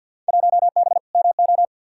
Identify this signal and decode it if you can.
Morse code, heard as a keyed tone, about 48 wpm, 700 Hz. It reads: 1B MO
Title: Field Day Exchanges - 50wpm